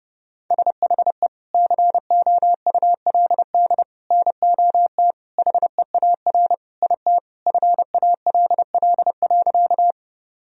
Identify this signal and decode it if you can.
Morse code, heard as a keyed tone, about 30 wpm, 700 Hz. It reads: SHE COULD NOT HEAR IT FALL.
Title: Sentences from Top 1000 Words - Review Part 2 - 30wpm